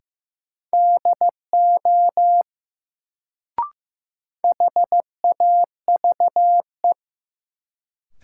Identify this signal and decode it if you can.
Morse code, heard as a keyed tone, about 15 wpm, 700 Hz. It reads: DO HAVE
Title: Sentences from Top 100 Words - New Word Drill - 15wpm